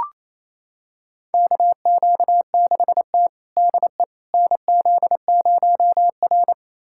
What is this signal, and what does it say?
Morse code, heard as a keyed tone, about 28 wpm, 700 Hz. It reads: KQ6T DE NZ0R
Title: USA Callsign DE Callsign - 4 Characters - 28wpm